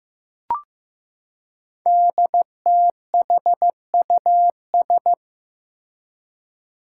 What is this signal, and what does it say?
Morse code, heard as a keyed tone, about 15 wpm, 700 Hz. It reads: DTHUS
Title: Five Challenging Letter-Number Combinations 15wpm